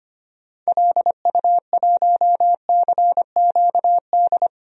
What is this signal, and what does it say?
Morse code, heard as a keyed tone, about 25 wpm, 700 Hz. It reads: LU1CQD
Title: DXCC-100 Call Signs - 6 Characters - 25wpm